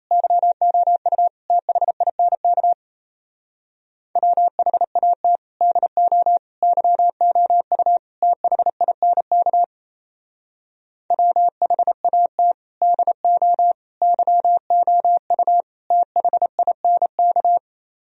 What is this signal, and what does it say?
Morse code, heard as a keyed tone, about 37 wpm, 700 Hz. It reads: YOU THINK WHAT DO YOU THINK WHAT DO YOU THINK